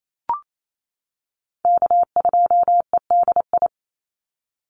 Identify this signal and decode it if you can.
Morse code, heard as a keyed tone, about 28 wpm, 700 Hz. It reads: K2EDI
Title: USA Call Signs 28wpm